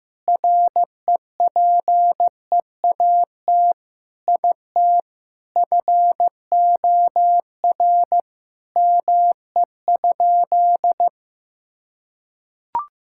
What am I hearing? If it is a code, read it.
Morse code, heard as a keyed tone, about 16 wpm, 700 Hz. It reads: REPEAT IT FOR ME?
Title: Sentences from Top 1600 Words 15wpm